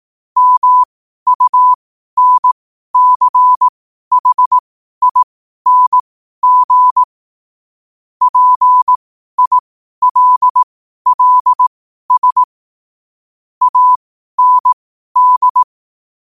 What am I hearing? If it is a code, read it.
Morse code, heard as a keyed tone, about 18 wpm, spaced as for 12 wpm, 1000 Hz. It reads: MUNCHING PILLS AND
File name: qotdfiles / 12wpm